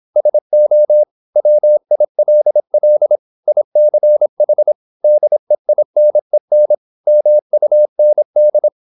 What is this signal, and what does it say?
Morse code, heard as a keyed tone, about 26 wpm, 600 Hz. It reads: SO WILL ICH DEINEN MUND